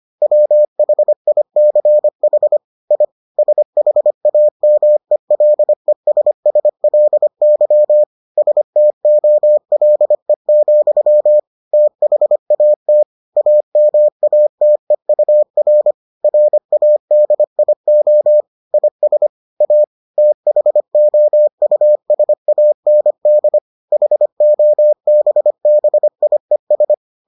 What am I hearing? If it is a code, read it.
Morse code, heard as a keyed tone, about 25 wpm, 600 Hz. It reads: WHICH I SHAMELESSLY STOLE, THAT AMATEUR RADIO IS A THOUSAND HOBBIES